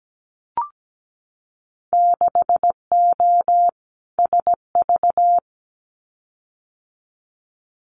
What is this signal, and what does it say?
Morse code, heard as a keyed tone, about 17 wpm, 700 Hz. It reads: 6O SV
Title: Winter Field Day Exchanges Encore 17wpm